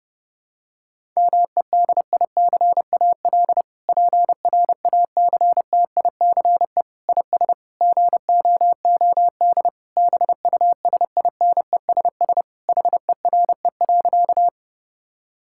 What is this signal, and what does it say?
Morse code, heard as a keyed tone, about 30 wpm, 700 Hz. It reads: MEDICAL PRACTICE IS GOOD BUSINESS HERE.
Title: Sentences from Top 1000 Words - Review Part 5 - 30wpm